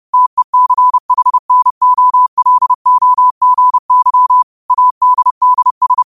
Morse code, heard as a keyed tone, about 30 wpm, 1000 Hz. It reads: TECHNOLOGY ADDS